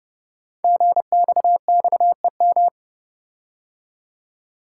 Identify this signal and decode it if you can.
Morse code, heard as a keyed tone, about 30 wpm, 690 Hz. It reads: GXXEM
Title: Code Groups - 5 Letters - 30wpm